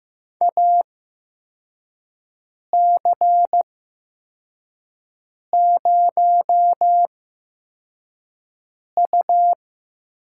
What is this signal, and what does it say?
Morse code, heard as a keyed tone, about 15 wpm, spaced as for 4 wpm, 700 Hz. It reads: AC0U
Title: USA Call Signs - 4 Characters - 8x Character Spacing - 15wpm